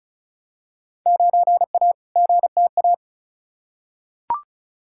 Morse code, heard as a keyed tone, about 34 wpm, 700 Hz. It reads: 9A GTA
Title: Field Day Exchanges Encore - 35wpm